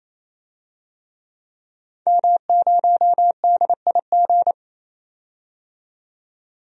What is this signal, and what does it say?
Morse code, heard as a keyed tone, about 28 wpm, 700 Hz. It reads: M0DIG
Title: DXCC-100 Call Signs - 5 Characters - 28wpm